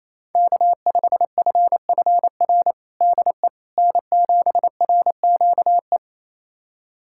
Morse code, heard as a keyed tone, about 28 wpm, 700 Hz. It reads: K5FFR DE N7RQE
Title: USA Callsign DE Callsign - 5 Characters - 28wpm